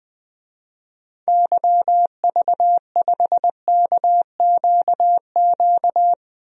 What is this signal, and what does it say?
Morse code, heard as a keyed tone, about 21 wpm, 700 Hz. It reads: YV5KQQ
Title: DXCC-100 Call Signs - 6 Characters - Repeated 3x - 20wpm